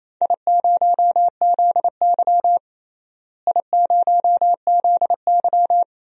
Morse code, heard as a keyed tone, about 28 wpm, 700 Hz. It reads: I0ZY I0ZY
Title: DXCC-100 Call Signs - 4 Characters - Repeated 3x - 28wpm